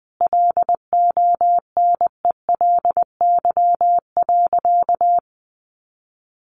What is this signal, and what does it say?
Morse code, heard as a keyed tone, about 20 wpm, 700 Hz. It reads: LONELY.
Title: Additional Easy Phrases 20wpm